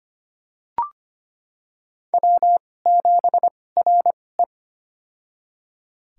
Morse code, heard as a keyed tone, about 25 wpm, spaced as for 18 wpm, 700 Hz. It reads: W7RE
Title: USA Call Signs - 4 Characters - 2x Character Spacing - 25wpm